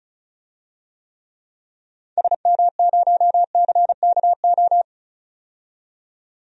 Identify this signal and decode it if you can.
Morse code, heard as a keyed tone, about 36 wpm, 700 Hz. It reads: SM0CKO
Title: DXCC-100 Call Signs - 6 Characters - 35wpm